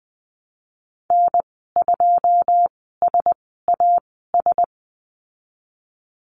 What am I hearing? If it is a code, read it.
Morse code, heard as a keyed tone, about 20 wpm, spaced as for 15 wpm, 700 Hz. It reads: N2SAS
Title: 088 Lesson 15g - Number 5 - Call Signs 20wpm